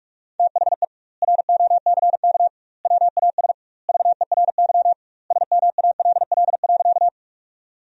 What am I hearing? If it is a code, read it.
Morse code, heard as a keyed tone, about 48 wpm, 700 Hz. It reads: THE ROCK WAS VERY SMALL.
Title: 2x Word-Spacing - Sentences from Top 400 Words - 45wpm